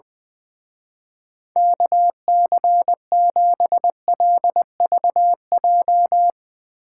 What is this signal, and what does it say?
Morse code, heard as a keyed tone, about 20 wpm, 700 Hz. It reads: KC7LVJ
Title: Speed Racing - USA 2x3 Call Signs 20wpm